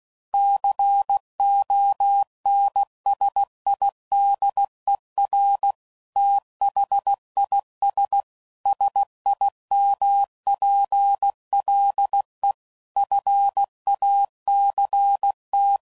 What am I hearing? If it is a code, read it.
Morse code, heard as a keyed tone, about 16 wpm, 800 Hz. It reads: CONSIDER THIS SIMPLE FACT
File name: CW Code Groups @15WPM / Technical Subjects / AND Gate